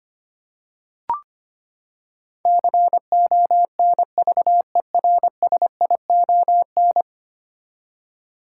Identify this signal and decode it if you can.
Morse code, heard as a keyed tone, about 25 wpm, 700 Hz. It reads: CONVERSION
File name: Ten Letter Words - 25wpm